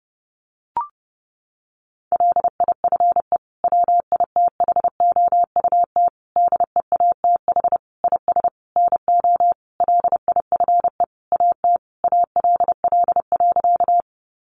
Morse code, heard as a keyed tone, about 30 wpm, 700 Hz. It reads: LIFE WITHOUT DEATH IS NO LIFE AT ALL.
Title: Sentences from Top 1100 Words 30wpm